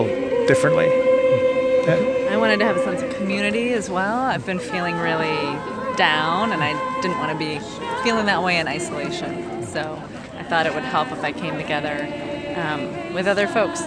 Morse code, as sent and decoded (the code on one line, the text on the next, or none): none